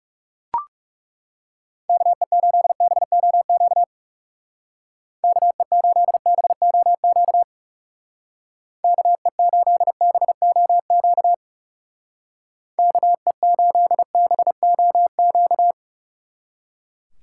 Morse code, heard as a keyed tone, about 37 wpm, 700 Hz. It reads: KE8BOQ KE8BOQ KE8BOQ KE8BOQ
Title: Speed Racing - USA 2x3 Call Signs 30wpm